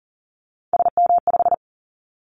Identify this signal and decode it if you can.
Morse code, heard as a keyed tone, about 40 wpm, 700 Hz. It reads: SM5
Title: Callsign Prefixes for Top 25 DXCC - 40wpm